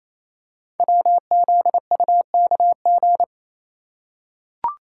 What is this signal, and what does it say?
Morse code, heard as a keyed tone, about 27 wpm, 700 Hz. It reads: WZUKG
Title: Code Groups - 5 Letters - 28wpm